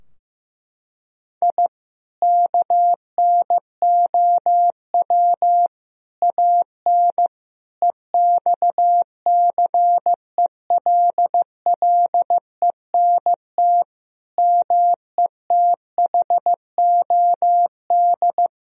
Morse code, heard as a keyed tone, about 15 wpm, 700 Hz. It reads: I KNOW AN EXCELLENT METHOD
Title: Sentences from Top 1000 Words - Review Part 3 - 15wpm